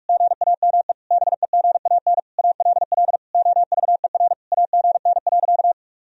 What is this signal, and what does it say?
Morse code, heard as a keyed tone, about 46 wpm, 700 Hz. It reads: GAME BEGAN ALL OVER AGN.
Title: Sentences from Top 500 Words and Abbreviations 45wpm